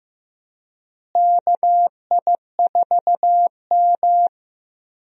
Morse code, heard as a keyed tone, about 15 wpm, 700 Hz. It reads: KI4M
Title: USA Call Signs - 4 Characters - Repeated 2x - 15wpm